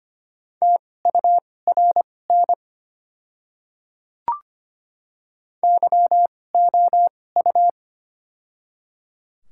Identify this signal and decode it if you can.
Morse code, heard as a keyed tone, about 25 wpm, spaced as for 18 wpm, 690 Hz. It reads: TURN YOU
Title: 2x Character Spacing - Top 300 Words - 25wpm